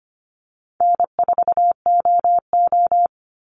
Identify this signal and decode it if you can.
Morse code, heard as a keyed tone, about 25 wpm, 700 Hz. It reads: N4OO
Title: Call Signs - 25wpm